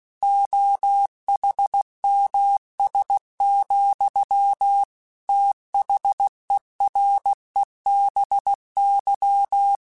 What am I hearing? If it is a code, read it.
Morse code, heard as a keyed tone, about 16 wpm, 800 Hz. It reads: OHMS, THEREBY